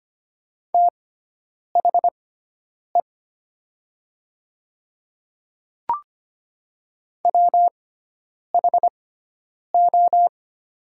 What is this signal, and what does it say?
Morse code, heard as a keyed tone, about 25 wpm, spaced as for 9 wpm, 700 Hz. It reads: THE WHO
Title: 6x Character Spacing - Top 200 Words - 25wpm